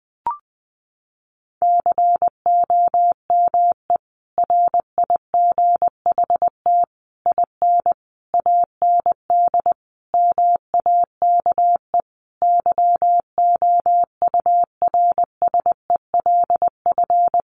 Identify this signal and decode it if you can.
Morse code, heard as a keyed tone, about 20 wpm, 700 Hz. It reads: COME RIGHT IN AND MAKE YOURSELF
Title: Sentences from Top 600 Words 20wpm